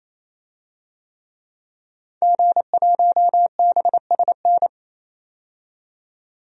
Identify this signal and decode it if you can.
Morse code, heard as a keyed tone, about 28 wpm, 700 Hz. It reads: G1BSN
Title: DXCC-100 Call Signs - 5 Characters - 28wpm